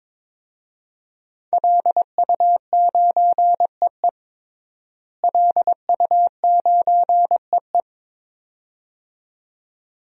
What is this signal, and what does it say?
Morse code, heard as a keyed tone, about 22 wpm, 700 Hz. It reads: LU9EE LU9EE
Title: DXCC-100 Call Signs - 5 Characters - Repeated 2x - 22wpm